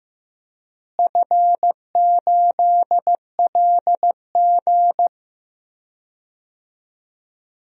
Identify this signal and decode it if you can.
Morse code, heard as a keyed tone, about 15 wpm, 700 Hz. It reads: F8LG
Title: DXCC-100 Call Signs - 4 Characters - Repeated 2x - 15wpm